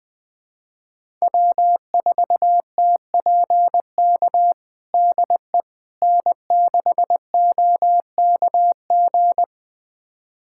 Morse code, heard as a keyed tone, about 20 wpm, 700 Hz. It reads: W4TPK DE N6OKG
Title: USA Callsign DE Callsign - 5 Characters - 20wpm